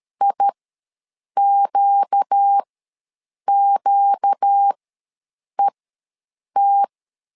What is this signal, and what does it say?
Morse code, heard as a keyed tone, about 13 wpm, spaced as for 7 wpm, 800 Hz. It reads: IQQET
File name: LEZIONI / 4 Q ws 12wpm 1912